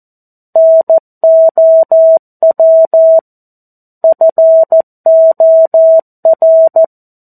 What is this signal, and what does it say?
Morse code, heard as a keyed tone, about 14 wpm, 650 Hz. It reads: NOW FOR
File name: 02-Chapter-02 14 WPM